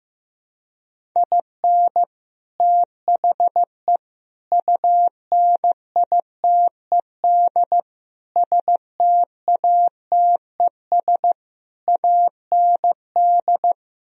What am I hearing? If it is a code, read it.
Morse code, heard as a keyed tone, about 15 wpm, 700 Hz. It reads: IN THE UNITED STATES AND
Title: Speed-Racing - Sets of 5 Words 15wpm